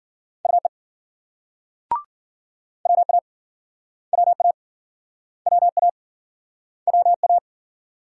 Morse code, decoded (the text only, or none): UE WA WA WA WA